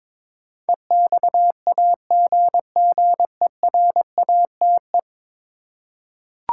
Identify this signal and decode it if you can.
Morse code, heard as a keyed tone, about 22 wpm, 700 Hz. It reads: EXAGGERATE
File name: practice / 22wpm / 10 Letter Words - First 3 Letters Given - 22wpm